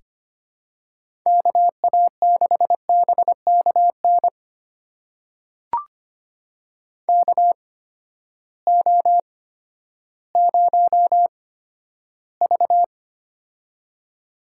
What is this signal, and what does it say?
Morse code, heard as a keyed tone, about 25 wpm, spaced as for 12 wpm, 700 Hz. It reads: KA6BKN KO0V
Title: USA Call Signs - 8x Character Spacing - 25wpm